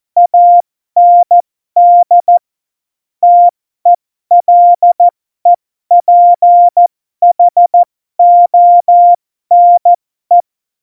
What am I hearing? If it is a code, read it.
Morse code, heard as a keyed tone, about 14 wpm, spaced as for 12 wpm, 700 Hz. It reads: AND TELEPHONE
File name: Morsekurs / Witze